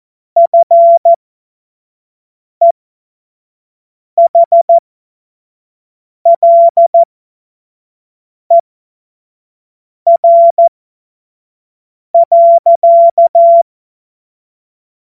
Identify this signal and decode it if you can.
Morse code, heard as a keyed tone, about 14 wpm, spaced as for 5 wpm, 680 Hz. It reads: FEHLER.
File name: Morsekurs / Witze / Witz 14.November 2023 25 F70